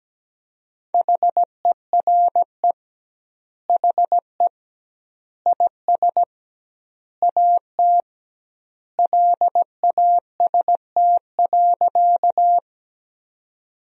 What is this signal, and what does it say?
Morse code, heard as a keyed tone, about 17 wpm, 700 Hz. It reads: HERE HE IS AT LAST.
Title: 2x Word-Spacing - Sentences from Top 300 Words - 17wpm